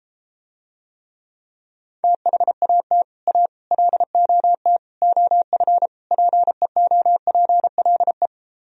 Morse code, heard as a keyed tone, about 33 wpm, 700 Hz. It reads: THAT A LOT OF PEOPLE